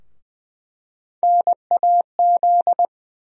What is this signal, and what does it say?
Morse code, heard as a keyed tone, about 20 wpm, 700 Hz. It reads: NAZ
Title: USA Call Sign Postfixes - 3 Letters 20wpm